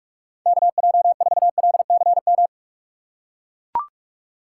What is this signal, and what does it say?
Morse code, heard as a keyed tone, about 44 wpm, 700 Hz. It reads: KJ4LCM
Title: USA Call Signs - 6 Characters - 6x Character Spacing - 45wpm